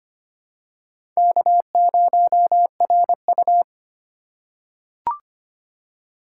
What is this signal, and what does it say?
Morse code, heard as a keyed tone, about 25 wpm, 700 Hz. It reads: K0RU
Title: USA Call Signs - 4 Characters - 4x Character Spacing - 25wpm